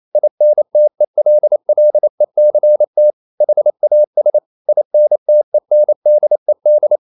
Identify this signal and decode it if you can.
Morse code, heard as a keyed tone, about 28 wpm, 600 Hz. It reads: INTELLECT HAS INTENDED
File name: CW / Practice / EnglishQuotations0128